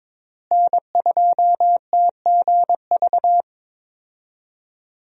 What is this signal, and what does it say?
Morse code, heard as a keyed tone, about 22 wpm, 700 Hz. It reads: N2TGV
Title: USA Call Signs - 5 Characters - Repeated 3x - 22wpm